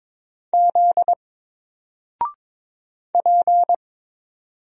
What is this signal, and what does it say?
Morse code, heard as a keyed tone, about 22 wpm, 700 Hz. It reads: Z P
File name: Single-Letter-Number-22wpm